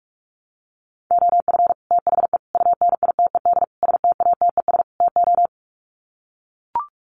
Morse code, heard as a keyed tone, about 44 wpm, 700 Hz. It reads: OF THE UNITED STATES TO
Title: Sets of 5 Words Encore 45wpm